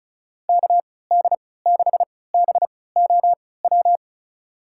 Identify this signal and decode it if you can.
Morse code, heard as a keyed tone, about 36 wpm, spaced as for 20 wpm, 700 Hz. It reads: KD6BOW
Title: USA Call Signs - 6 Characters - 3x Character Spacing - 35wpm